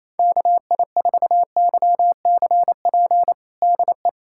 Morse code, heard as a keyed tone, about 28 wpm, 700 Hz. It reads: KI4YCP DE
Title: USA Callsign DE Callsign - 6 Characters - 28wpm